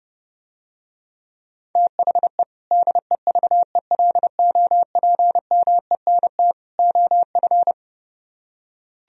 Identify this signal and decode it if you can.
Morse code, heard as a keyed tone, about 30 wpm, 700 Hz. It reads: THE DEVELOPMENT OF